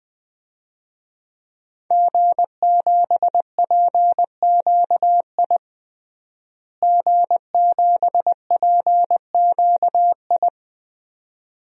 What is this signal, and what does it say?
Morse code, heard as a keyed tone, about 21 wpm, 700 Hz. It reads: G7PQI G7PQI
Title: DXCC-100 Call Signs - 5 Characters - Repeated 2x - 20wpm